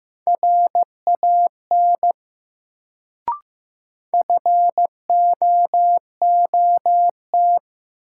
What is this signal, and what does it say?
Morse code, heard as a keyed tone, about 15 wpm, 700 Hz. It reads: RAN FOOT
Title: Sentences from Top 500 Words - New Word Drill - 15wpm